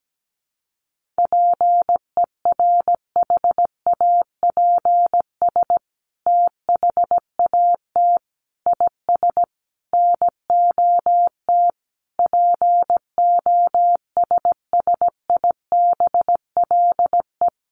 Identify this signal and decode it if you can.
Morse code, heard as a keyed tone, about 17 wpm, 700 Hz. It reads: PERHAPS THAT IS NOT POSSIBLE
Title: Sentences from Top 900 Words 17wpm